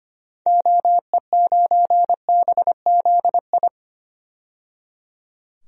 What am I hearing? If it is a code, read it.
Morse code, heard as a keyed tone, about 25 wpm, 700 Hz. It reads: OE9BZI
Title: DXCC-100 Call Signs - 6 Characters - 25wpm